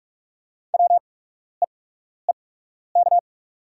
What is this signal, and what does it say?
Morse code, heard as a keyed tone, about 44 wpm, spaced as for 12 wpm, 700 Hz. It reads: WEEK
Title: 8x Character Spacing - Top 500 Words - 45wpm